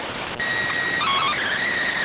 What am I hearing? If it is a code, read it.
SSTV (slow-scan television), PD 290.